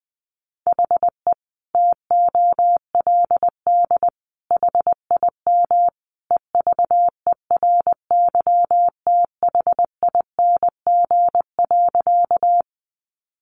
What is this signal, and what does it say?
Morse code, heard as a keyed tone, about 20 wpm, 700 Hz. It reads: HE TOLD HIM EVERYTHING.
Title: Sentences from Top 700 Words 20wpm